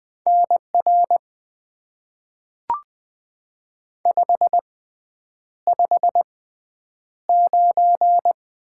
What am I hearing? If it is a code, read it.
Morse code, heard as a keyed tone, about 20 wpm, spaced as for 9 wpm, 700 Hz. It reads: NR 559
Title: Top 90 QSO Elements - 6x Character Spacing - 20wpm